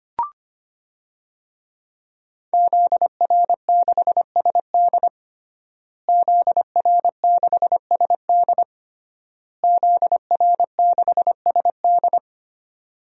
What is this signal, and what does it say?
Morse code, heard as a keyed tone, about 25 wpm, 700 Hz. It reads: ZR6SD ZR6SD ZR6SD
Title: DXCC-100 Call Signs - 5 Characters - Repeated 3x - 25wpm